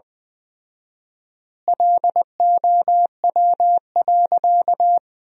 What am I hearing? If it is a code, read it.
Morse code, heard as a keyed tone, about 20 wpm, 700 Hz. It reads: LOW.
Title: 4x Word-Spacing - Sentences from Top 500 Words - 20wpm